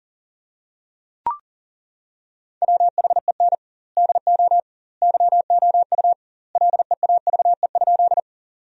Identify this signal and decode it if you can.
Morse code, heard as a keyed tone, about 37 wpm, 700 Hz. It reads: WHEN DO YOU LEAVE?
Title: 2x Word-Spacing - Sentences from Top 400 Words - 40wpm